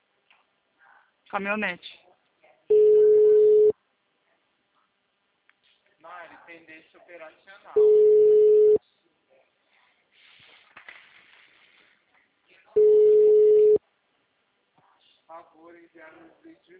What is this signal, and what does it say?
Ringback tone (Europe, 425 Hz), with speech.
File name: out- -2008- - - .11433